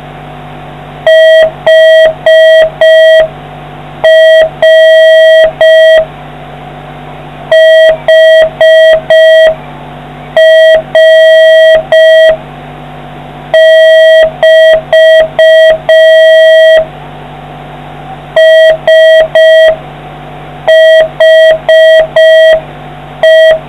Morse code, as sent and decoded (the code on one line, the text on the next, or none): .... .-. / .... .-. -...- / ... .... .
HR HR<BT> SHE